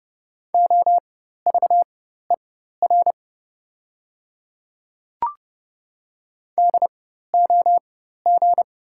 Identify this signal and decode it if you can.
Morse code, heard as a keyed tone, about 30 wpm, spaced as for 14 wpm, 700 Hz. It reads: OVER DOG
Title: 4x Character Spacing - Top 500 Words - 30wpm